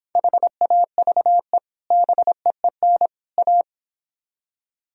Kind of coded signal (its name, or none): Morse code